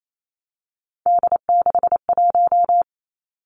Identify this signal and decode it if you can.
Morse code, heard as a keyed tone, about 28 wpm, 690 Hz. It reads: D61
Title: Three Challenging Letter-Number Combinations 28wpm